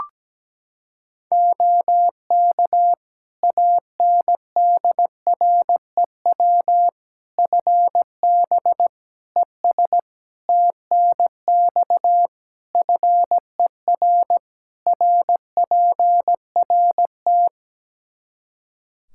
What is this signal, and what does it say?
Morse code, heard as a keyed tone, about 17 wpm, 700 Hz. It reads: OK ANDREW FB ES TNX FER RPRT
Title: Practice QSO - Exchange 2 - Info - 17wpm